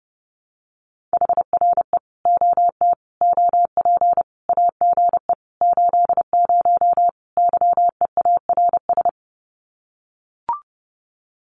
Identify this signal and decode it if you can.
Morse code, heard as a keyed tone, about 30 wpm, 700 Hz. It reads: HRE OT OP AGE 80 YEARS
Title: Common QSO Phrases 30wpm